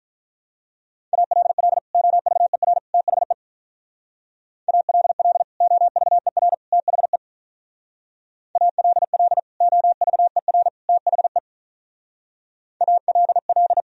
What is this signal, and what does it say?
Morse code, heard as a keyed tone, about 43 wpm, 700 Hz. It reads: ALL OVER THE ALL OVER THE ALL OVER THE ALL
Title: Speed-Racing - Sets of 3 Words 35wpm